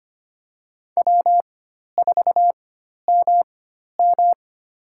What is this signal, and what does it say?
Morse code, heard as a keyed tone, about 25 wpm, spaced as for 12 wpm, 700 Hz. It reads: W4MM
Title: USA Call Signs - 4 Characters - 4x Character Spacing - 25wpm